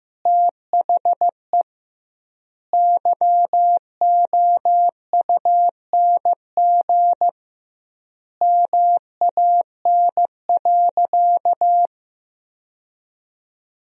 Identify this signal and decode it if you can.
Morse code, heard as a keyed tone, about 15 wpm, 700 Hz. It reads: THE YOUNG MAN.
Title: 2x Word-Spacing - Sentences from Top 400 Words - 15wpm